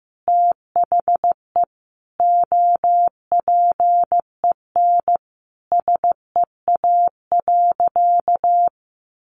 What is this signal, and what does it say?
Morse code, heard as a keyed tone, about 15 wpm, 700 Hz. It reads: THE OPEN SEA.
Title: Sentences from Top 1500 Words 15wpm